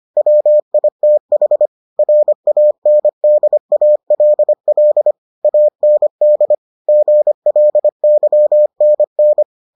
Morse code, heard as a keyed tone, about 25 wpm, 600 Hz. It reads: WITH RANDALL AND GLYNN